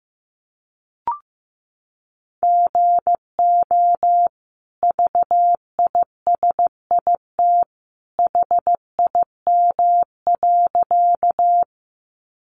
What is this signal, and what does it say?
Morse code, heard as a keyed tone, about 15 wpm, 700 Hz. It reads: GO VISIT HIM.
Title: Sentences from Top 900 Words 15wpm